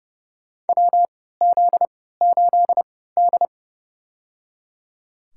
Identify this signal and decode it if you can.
Morse code, heard as a keyed tone, about 30 wpm, spaced as for 17 wpm, 700 Hz. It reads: WZ8D
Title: USA Call Signs - 4 Characters - 3x Character Spacing - 30wpm